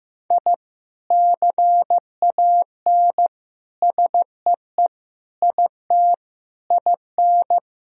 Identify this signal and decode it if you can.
Morse code, heard as a keyed tone, about 15 wpm, 700 Hz. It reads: I CAN SEE IT IN
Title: Sentences from Top 1000 Words - Review Part 2 - 15wpm